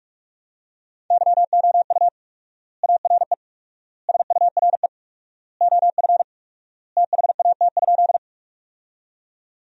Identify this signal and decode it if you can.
Morse code, heard as a keyed tone, about 46 wpm, 700 Hz. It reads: YOU ARE SURE OF THAT?
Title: 4x Word-Spacing - Sentences from Top 300 Words - 45wpm